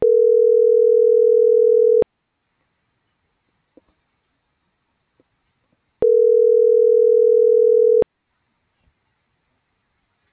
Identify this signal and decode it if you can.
Ringback tone (North America), 440 + 480 Hz.
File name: CENTRAL GOMSA / out alan.escudero